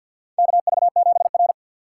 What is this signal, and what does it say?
Morse code, heard as a keyed tone, about 53 wpm, 700 Hz. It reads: KV7R
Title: USA Call Signs - 4 Characters - 3x Character Spacing - 50wpm